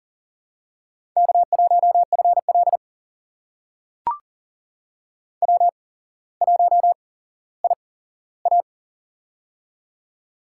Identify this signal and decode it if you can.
Morse code, heard as a keyed tone, about 40 wpm, spaced as for 17 wpm, 700 Hz. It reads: K1FL W1IA
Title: USA Call Signs - 4 Characters - 8x Character Spacing - 40wpm